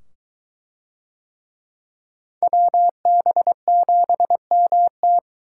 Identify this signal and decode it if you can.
Morse code, heard as a keyed tone, about 21 wpm, 700 Hz. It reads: WB7MT